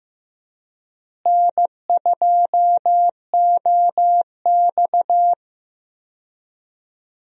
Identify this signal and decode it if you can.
Morse code, heard as a keyed tone, about 15 wpm, 700 Hz. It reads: N2OX
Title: USA Call Signs - 4 Characters - Repeated 3x - 15wpm